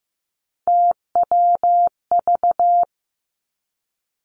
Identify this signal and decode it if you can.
Morse code, heard as a keyed tone, about 15 wpm, 700 Hz. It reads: TWV
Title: Three Challenging Letter-Number Combinations 15wpm